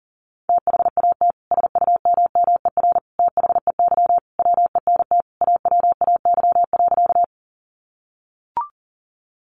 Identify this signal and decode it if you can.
Morse code, heard as a keyed tone, about 40 wpm, 700 Hz. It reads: THAT SUMMER THEY WENT AWAY.
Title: Sentences from Top 1100 Words 40wpm